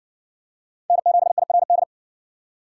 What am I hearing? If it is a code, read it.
Morse code, heard as a keyed tone, about 54 wpm, 700 Hz. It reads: N7IRD